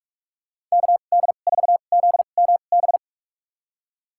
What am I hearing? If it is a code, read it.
Morse code, heard as a keyed tone, about 46 wpm, spaced as for 32 wpm, 700 Hz. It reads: KD4ZMB
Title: USA Call Signs - 6 Characters - 2x Character Spacing - 45wpm